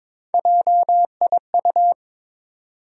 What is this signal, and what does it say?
Morse code, heard as a keyed tone, about 22 wpm, 700 Hz. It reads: JIU
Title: USA Call Sign Postfixes - 3 Letters 22wpm